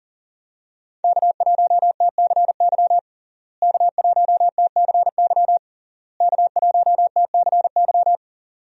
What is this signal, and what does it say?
Morse code, heard as a keyed tone, about 40 wpm, 700 Hz. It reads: K1TCY K1TCY K1TCY
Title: USA Call Signs - 5 Characters - Repeated 3x - 40wpm